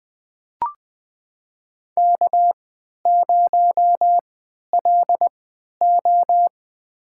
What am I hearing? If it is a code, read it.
Morse code, heard as a keyed tone, about 20 wpm, spaced as for 11 wpm, 700 Hz. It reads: K0LO
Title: USA Call Signs - 4 Characters - 3x Character Spacing - 20wpm